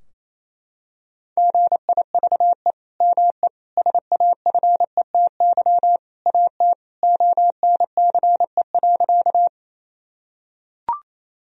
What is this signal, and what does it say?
Morse code, heard as a keyed tone, about 28 wpm, 700 Hz. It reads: GIVE ME SAFETY AT ONCE.
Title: Sentences from Top 1000 Words - Review Part 4 - 28wpm